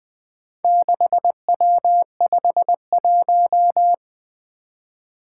Morse code, heard as a keyed tone, about 20 wpm, 700 Hz. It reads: 6W51
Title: Four Challenging Letter-Number Combinations 20wpm